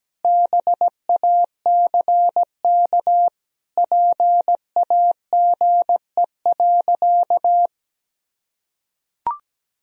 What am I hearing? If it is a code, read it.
Morse code, heard as a keyed tone, about 17 wpm, 700 Hz. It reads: BACK PAGE.
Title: Sentences from Top 1000 Words - Review Part 1 - 17wpm